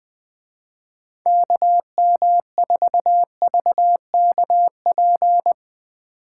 Morse code, heard as a keyed tone, about 19 wpm, 700 Hz. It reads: KM4VKP
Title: Speed Racing - USA 2x3 Call Signs 17wpm